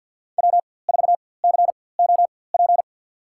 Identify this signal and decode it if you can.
Morse code, heard as a keyed tone, about 47 wpm, spaced as for 23 wpm, 700 Hz. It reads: W4COP
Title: USA Call Signs - 5 Characters - 4x Character Spacing - 50wpm